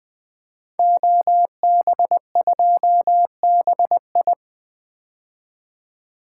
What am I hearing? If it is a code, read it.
Morse code, heard as a keyed tone, about 20 wpm, 700 Hz. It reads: OB2BI
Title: Code Groups - 5 Letter:Numbers - 20wpm